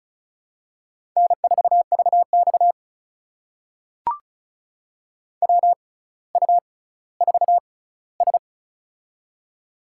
Morse code, heard as a keyed tone, about 36 wpm, spaced as for 18 wpm, 700 Hz. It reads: N4VX WU4S
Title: USA Call Signs - 4 Characters - 6x Character Spacing - 35wpm